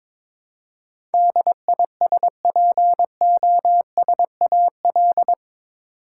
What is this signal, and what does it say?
Morse code, heard as a keyed tone, about 22 wpm, 700 Hz. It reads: DISPOSAL
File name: practice / 22wpm / Eight Letter Words - 22wpm